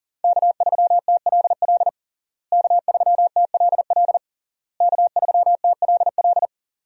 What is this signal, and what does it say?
Morse code, heard as a keyed tone, about 37 wpm, 700 Hz. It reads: K3TLL K3TLL K3TLL
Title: USA Call Signs - 5 Characters - Repeated 3x - 40wpm